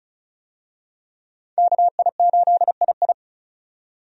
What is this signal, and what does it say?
Morse code, heard as a keyed tone, about 36 wpm, 700 Hz. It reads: KI8II
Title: USA Call Signs - 5 Characters - 35wpm